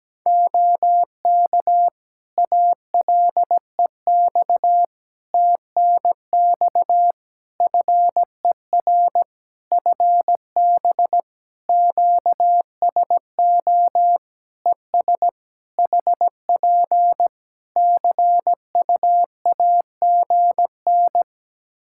Morse code, heard as a keyed tone, about 17 wpm, 700 Hz. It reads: OK ALEX TNX FER FB QSO ES HP CUAGN
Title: Practice QSO - Exchange 4 - Ending - 17wpm